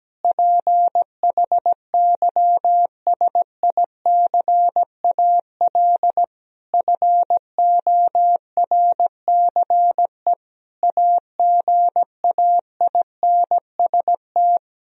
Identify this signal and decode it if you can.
Morse code, heard as a keyed tone, about 17 wpm, 700 Hz. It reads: PHYSICAL FORCE AGAINST
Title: Sentences from Top 1000 Words - Review Part 4 - 17wpm